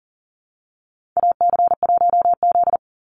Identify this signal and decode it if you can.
Morse code, heard as a keyed tone, about 40 wpm, 700 Hz. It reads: AC1Z
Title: NA SOTA Chaser Callsigns 40wpm